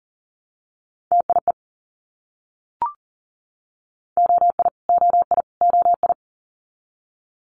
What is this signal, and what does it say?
Morse code, heard as a keyed tone, about 41 wpm, 700 Hz. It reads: TIE OI OI OI
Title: Multi-letter Phonemes Repeated 3x 40wpm